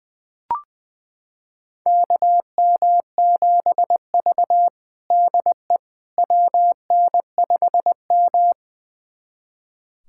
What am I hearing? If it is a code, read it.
Morse code, heard as a keyed tone, about 20 wpm, 700 Hz. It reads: KM7V DE WN5M
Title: USA Callsign DE Callsign - 4 Characters - 20wpm